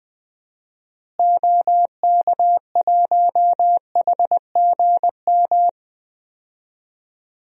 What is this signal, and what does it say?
Morse code, heard as a keyed tone, about 21 wpm, 700 Hz. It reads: OK1HGM
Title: DXCC-100 Call Signs - 6 Characters - Repeated 2x - 20wpm